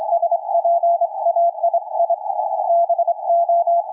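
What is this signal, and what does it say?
Morse code, heard as a keyed tone, about 26 wpm, 680 Hz, with radio noise. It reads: SPAII BO
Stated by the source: QRP station's CW heard around 7029.2 kHz; signal fading from a solid 559 down to complete dropout.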